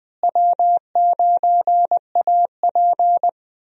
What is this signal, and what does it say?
Morse code, heard as a keyed tone, about 20 wpm, 700 Hz. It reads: W9AP
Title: USA Call Signs - 4 Characters - Repeated 3x - 20wpm